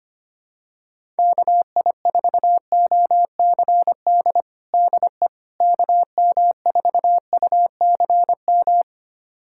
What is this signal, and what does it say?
Morse code, heard as a keyed tone, about 25 wpm, 700 Hz. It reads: KI4OCD DE KM4UCM
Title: USA Callsign DE Callsign - 6 Characters - 25wpm